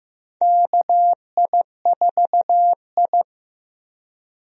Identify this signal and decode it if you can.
Morse code, heard as a keyed tone, about 15 wpm, 700 Hz. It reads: KI4I
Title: USA Call Signs - 4 Characters - Repeated 2x - 15wpm